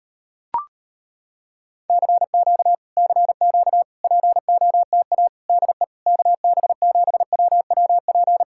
Morse code, heard as a keyed tone, about 35 wpm, 700 Hz. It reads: CQ CQ POTA DE KB7WWP